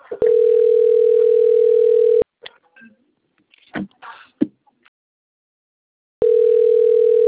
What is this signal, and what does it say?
Ringback tone (North America), 440 + 480 Hz.
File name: rg-1600-unknown- - - .58165